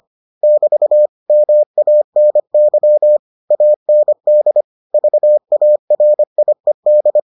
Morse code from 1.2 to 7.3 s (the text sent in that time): MANY AND VARIED